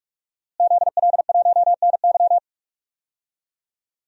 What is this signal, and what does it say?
Morse code, heard as a keyed tone, about 46 wpm, 700 Hz. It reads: ZL1NY
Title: DXCC-100 Call Signs - 5 Characters - Repeated 3x - 45wpm